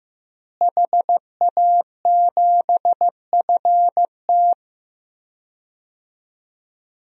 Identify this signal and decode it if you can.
Morse code, heard as a keyed tone, about 16 wpm, 700 Hz. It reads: HA7FT
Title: DXCC-100 Call Signs - 5 Characters - Repeated 3x - 15wpm